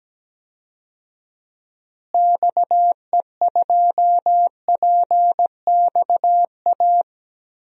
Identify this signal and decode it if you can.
Morse code, heard as a keyed tone, about 17 wpm, 700 Hz. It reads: XE2PXA
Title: DXCC-100 Call Signs - 6 Characters - Repeated 3x - 17wpm